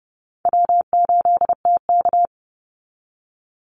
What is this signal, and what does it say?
Morse code, heard as a keyed tone, about 30 wpm, 700 Hz. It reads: W8TK
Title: Easy Call Signs Repeated 2x - 30wpm